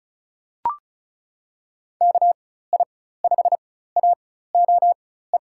Morse code, heard as a keyed tone, about 34 wpm, spaced as for 16 wpm, 700 Hz. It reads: KI5AOE
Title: USA Call Signs - 6 Characters - 4x Character Spacing - 35wpm